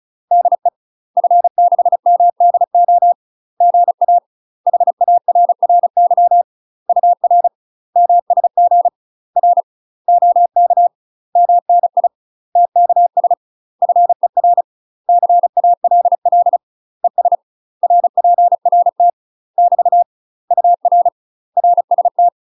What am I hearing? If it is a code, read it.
Morse code, heard as a keyed tone, about 36 wpm, 700 Hz. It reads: DE F6MDO GA HARRY UR MSG R OK MNI TKS FER CALL ES RPRT <BT> UR RST
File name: UK MASTER QSOs TRAINING / 35WPM / UK MASTER QSOs 0031